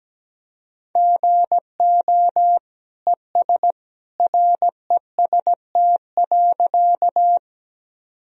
Morse code, heard as a keyed tone, about 17 wpm, 700 Hz. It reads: GO ES REST.
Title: Sentences from Top 500 Words and Abbreviations 17wpm